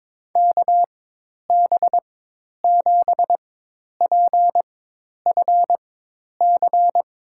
Morse code, heard as a keyed tone, about 21 wpm, spaced as for 10 wpm, 700 Hz. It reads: KB7PFC
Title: USA Call Signs - 4x Character Spacing - 22wpm